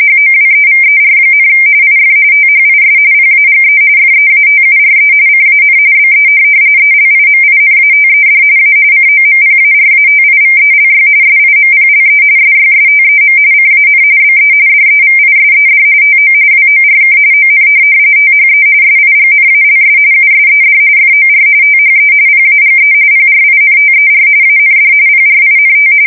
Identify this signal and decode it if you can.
RTTY, about 45.45 baud. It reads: HOWEVER, IF THIS PAGE FEATURE CAN BE ELIMINATED A COMPARATIVELY SIMPLE SYSTEM CAN BE DEVISED TO WORK ON THE SIGNALS USED FOR THE PAGE PRINTER AND PRINT
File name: fsk170 / printing-telegraph